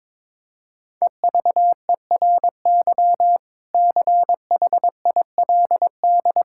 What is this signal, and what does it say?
Morse code, heard as a keyed tone, about 22 wpm, 700 Hz. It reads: EVERY CHILD
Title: Sentences from Top 1000 Words - Review Part 6 - 22wpm